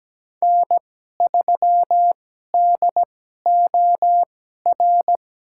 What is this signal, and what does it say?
Morse code, heard as a keyed tone, about 17 wpm, spaced as for 12 wpm, 700 Hz. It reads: N3DOR
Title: USA Call Signs - 5 Characters - 2x Character Spacing - 17wpm